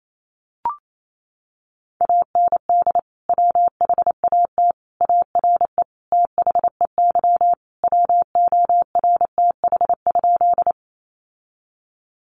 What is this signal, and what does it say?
Morse code, heard as a keyed tone, about 28 wpm, 700 Hz. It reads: AND WHAT ARE THEY WORTH?
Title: Sentences from Top 800 Words 28wpm